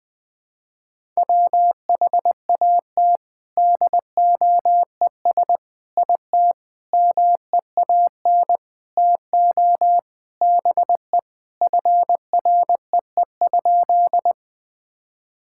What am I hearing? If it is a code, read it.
Morse code, heard as a keyed tone, about 20 wpm, 700 Hz. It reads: WHAT DOES IT MEAN TO BE FREE?
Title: Sentences with 6-Word Phrases - Spoken Beginning - 20wpm